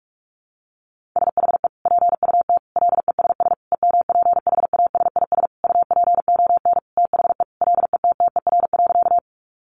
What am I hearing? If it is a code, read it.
Morse code, heard as a keyed tone, about 44 wpm, 700 Hz. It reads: SHE PUT LESS EMPHASIS UPON THE LETTER.
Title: Sentences from Top 1400 Words 45wpm